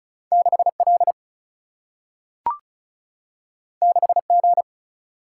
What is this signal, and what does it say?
Morse code, heard as a keyed tone, about 36 wpm, 700 Hz. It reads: 6L 6G
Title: Single-Letter Single-Number 35wpm